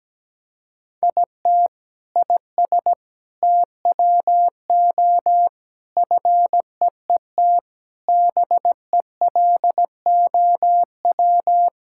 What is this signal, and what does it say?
Morse code, heard as a keyed tone, about 17 wpm, 700 Hz. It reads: IT IS TWO FEET BELOW
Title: Sentences from Top 1000 Words - Review Part 5 - 17wpm